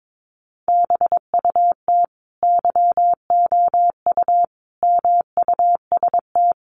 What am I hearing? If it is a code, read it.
Morse code, heard as a keyed tone, about 22 wpm, 700 Hz. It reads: BUT YOU MUST
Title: Sentences from Top 600 Words 22wpm